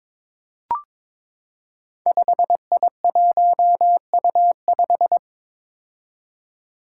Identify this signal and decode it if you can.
Morse code, heard as a keyed tone, about 22 wpm, 700 Hz. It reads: 5I1U5
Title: Five Challenging Letter-Number Combinations 22wpm